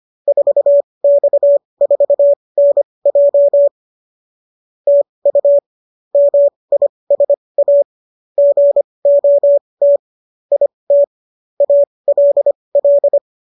Morse code, heard as a keyed tone, about 25 wpm, spaced as for 20 wpm, 570 Hz. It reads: 4X4NJ TU MISA GOT IT ALL